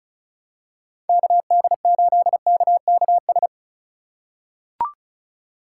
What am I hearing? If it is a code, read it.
Morse code, heard as a keyed tone, about 36 wpm, 700 Hz. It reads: KD8KKS
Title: USA Call Signs - 6 Characters - 8x Character Spacing - 35wpm